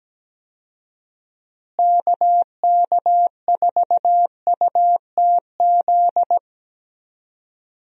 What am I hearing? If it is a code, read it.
Morse code, heard as a keyed tone, about 17 wpm, 700 Hz. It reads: KK4UTZ
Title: USA Call Signs - 6 Characters - Repeated 2x - 17wpm